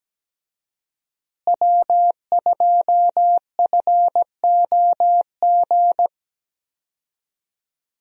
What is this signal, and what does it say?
Morse code, heard as a keyed tone, about 17 wpm, 700 Hz. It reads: W2FOG
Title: USA Call Signs - 5 Characters - Repeated 3x - 17wpm